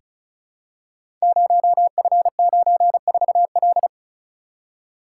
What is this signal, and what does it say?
Morse code, heard as a keyed tone, about 36 wpm, 700 Hz. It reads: 0F94L
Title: Code Groups - 5 Letter:Numbers - 35wpm